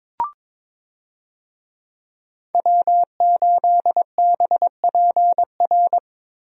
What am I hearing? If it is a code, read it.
Morse code, heard as a keyed tone, about 22 wpm, 700 Hz. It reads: W8BPR
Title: USA Call Signs - 5 Characters - Repeated 3x - 22wpm